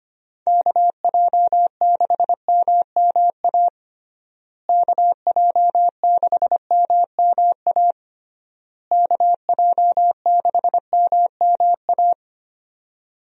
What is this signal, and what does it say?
Morse code, heard as a keyed tone, about 25 wpm, 700 Hz. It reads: KJ6MMA KJ6MMA KJ6MMA
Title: USA Call Signs - 6 Characters - Repeated 3x - 25wpm